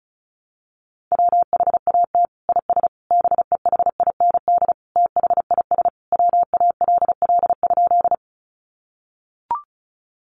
Morse code, heard as a keyed tone, about 34 wpm, 700 Hz. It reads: WHAT IS BEHIND THIS WALL?
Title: Sentences from Top 1100 Words 35wpm